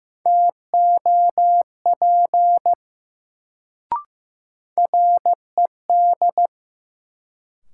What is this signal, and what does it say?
Morse code, heard as a keyed tone, about 15 wpm, 700 Hz. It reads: TOP RED
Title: Sentences from Top 700 Words - New Word Drill - 15wpm